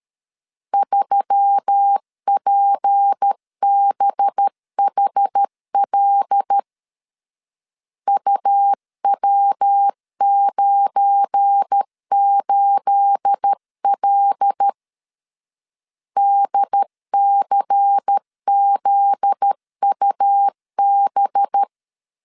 Morse code, heard as a keyed tone, about 13 wpm, 800 Hz. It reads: 3PBHL UW98L DCZUB
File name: LEZIONI / 9 28 12wpm 2351